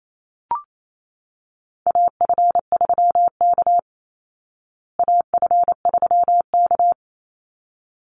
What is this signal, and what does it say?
Morse code, heard as a keyed tone, about 28 wpm, 700 Hz. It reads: AF3K AF3K
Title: Call Signs Repeated 3x - 28wpm